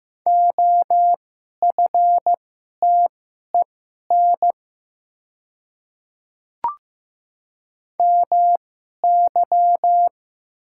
Morse code, heard as a keyed tone, about 15 wpm, spaced as for 11 wpm, 700 Hz. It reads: OFTEN MY
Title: 2x Character Spacing - Top 500 Words - 15wpm